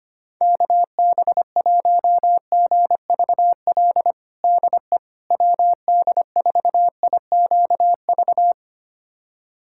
Morse code, heard as a keyed tone, about 25 wpm, 700 Hz. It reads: KB1GVL DE WD4IQV
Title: USA Callsign DE Callsign - 6 Characters - 25wpm